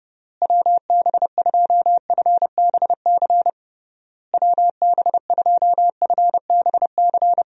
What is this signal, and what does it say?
Morse code, heard as a keyed tone, about 30 wpm, 700 Hz. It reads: WB2FBC WB2FBC
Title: USA Call Signs - 6 Characters - Repeated 3x - 30wpm